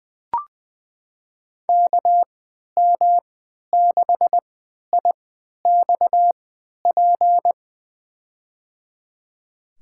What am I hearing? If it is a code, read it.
Morse code, heard as a keyed tone, about 20 wpm, spaced as for 11 wpm, 700 Hz. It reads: KM6IXP
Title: USA Call Signs - 3x Character Spacing - 20wpm